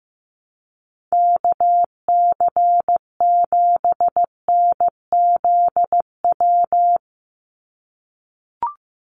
Morse code, heard as a keyed tone, about 15 wpm, 700 Hz. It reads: KC7NZW
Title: USA Call Signs 15wpm